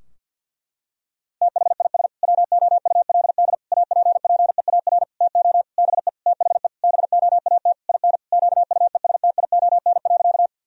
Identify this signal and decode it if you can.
Morse code, heard as a keyed tone, about 48 wpm, 700 Hz. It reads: THIS WOULD APPEAR TO BE THE BOAT IN QUESTION.
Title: Sentences from Top 1000 Words - Review Part 3 - 50wpm